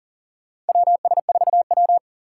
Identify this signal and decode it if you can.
Morse code, heard as a keyed tone, about 37 wpm, 700 Hz. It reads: WS4W
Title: USA Call Signs - 4 Characters - 6x Character Spacing - 40wpm